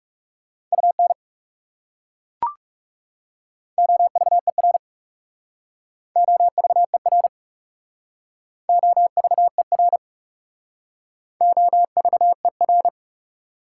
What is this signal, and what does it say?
Morse code, heard as a keyed tone, about 35 wpm, 700 Hz. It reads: UN OVER OVER OVER OVER
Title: Speed-Racing Prefixes and Suffixes 30wpm